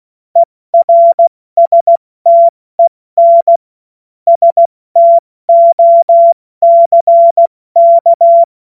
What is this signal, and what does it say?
Morse code, heard as a keyed tone, about 16 wpm, 680 Hz. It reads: ERSTEN STOCK
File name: Morsekurs / Witze / Witz 31. August 2021, 70 F80